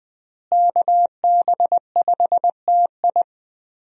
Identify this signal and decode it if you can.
Morse code, heard as a keyed tone, about 20 wpm, 700 Hz. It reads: KB5TI
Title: USA Call Signs - 5 Characters - Repeated 2x - 20wpm